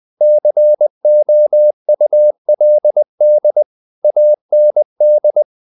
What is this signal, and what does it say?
Morse code, heard as a keyed tone, about 20 wpm, 600 Hz. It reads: COULD AND
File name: the old man and the sea part 2 20wpm